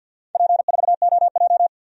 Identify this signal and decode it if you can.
Morse code, heard as a keyed tone, about 47 wpm, 700 Hz. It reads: P4OJ
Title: DXCC-100 Call Signs - 4 Characters - Repeated 3x - 50wpm